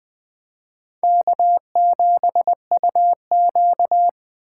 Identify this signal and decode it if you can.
Morse code, heard as a keyed tone, about 24 wpm, 700 Hz. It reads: K7UQ
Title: Speed Racing - USA Call Signs 20wpm